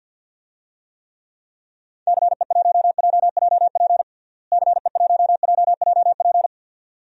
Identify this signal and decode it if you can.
Morse code, heard as a keyed tone, about 50 wpm, 700 Hz. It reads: CE1JJP CE1JJP
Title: DXCC-100 Call Signs - 6 Characters - Repeated 2x - 50wpm